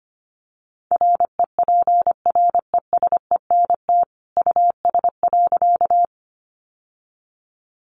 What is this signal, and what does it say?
Morse code, heard as a keyed tone, about 25 wpm, 700 Hz. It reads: REPRESENT US.
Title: Sentences from Top 1200 Words 25wpm